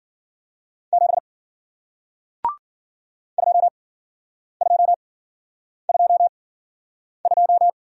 Morse code, heard as a keyed tone, about 56 wpm, spaced as for 9 wpm, 700 Hz. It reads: 7 2222